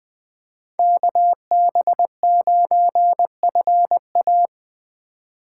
Morse code, heard as a keyed tone, about 23 wpm, 700 Hz. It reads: KB9FA
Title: Speed Racing - USA 2x2 Call Signs 20wpm